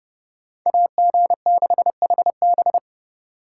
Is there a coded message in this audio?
Yes: Morse code from 0.7 to 2.8 s, heard as a keyed tone, about 29 wpm, 700 Hz. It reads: AG6HB